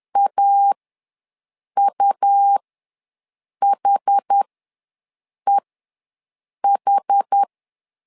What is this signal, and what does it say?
Morse code, heard as a keyed tone, about 10 wpm, spaced as for 6 wpm, 800 Hz. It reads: AUHEH
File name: LEZIONI / 2 UHDN / 2 UH ws 10wpm 2029